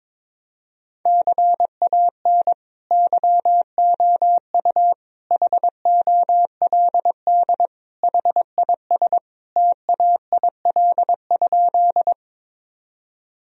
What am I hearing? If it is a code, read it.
Morse code, heard as a keyed tone, about 21 wpm, 700 Hz. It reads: CAN YOU HOLD HIS TAIL?
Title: Sentences from Top 1000 Words - Review Part 3 - 22wpm